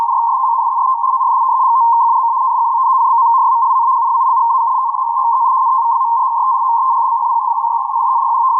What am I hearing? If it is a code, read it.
Two-tone teleprinter signal, about 80 Hz shift.